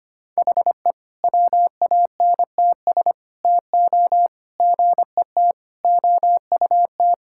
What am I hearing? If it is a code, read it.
Morse code, heard as a keyed tone, about 25 wpm, 700 Hz. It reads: HE WANTS TO GET OUT
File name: practice / 25wpm / Sentences with 4-Word Phrases - 25wpm